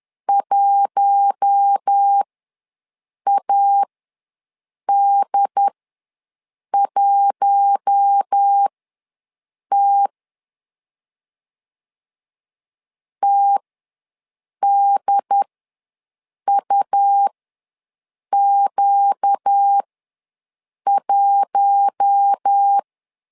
Morse code, heard as a keyed tone, about 10 wpm, spaced as for 6 wpm, 800 Hz. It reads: 1AD1T TDUQ1
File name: LEZIONI / 4 Q1 ws 10wpm 2538